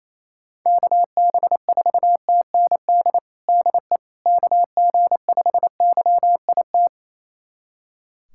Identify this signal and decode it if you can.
Morse code, heard as a keyed tone, about 28 wpm, 700 Hz. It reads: KB4TND DE KG5YIT
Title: USA Callsign DE Callsign - 6 Characters - 28wpm